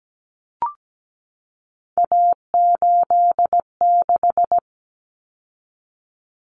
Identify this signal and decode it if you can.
Morse code, heard as a keyed tone, about 17 wpm, 700 Hz. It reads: A86
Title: Common Cut Numbers - 3-Digit 17wpm